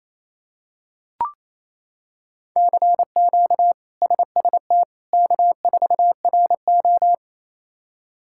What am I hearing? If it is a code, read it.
Morse code, heard as a keyed tone, about 28 wpm, 700 Hz. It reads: CQ SST K4RO
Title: K1USN SST QSO Practice - 28wpm